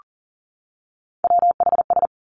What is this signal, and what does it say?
Morse code, heard as a keyed tone, about 42 wpm, 700 Hz. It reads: WHS
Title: Three Challenging Letter-Number Combinations 40wpm